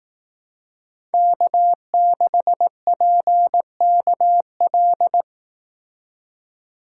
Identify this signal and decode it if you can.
Morse code, heard as a keyed tone, about 19 wpm, 700 Hz. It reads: K6PKL
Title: Speed Racing - USA 1x3 Call Signs 15wpm